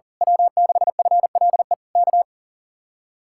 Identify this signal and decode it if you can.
Morse code, heard as a keyed tone, about 37 wpm, 700 Hz. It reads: W6FLE K
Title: Speed-Racing Common QSO Phrases Encore - 40wpm